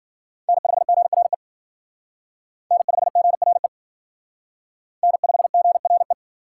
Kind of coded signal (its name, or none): Morse code